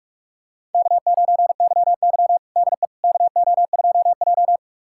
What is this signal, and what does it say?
Morse code, heard as a keyed tone, about 46 wpm, 700 Hz. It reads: K9YY DE KO2J
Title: USA Callsign DE Callsign - 4 Characters - 45wpm